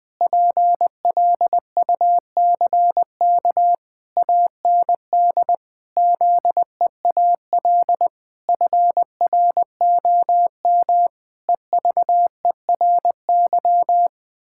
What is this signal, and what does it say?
Morse code, heard as a keyed tone, about 20 wpm, 700 Hz. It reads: PLUCK AND ZEAL FROM EVERY
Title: Pangram Challenge v2 - 20wpm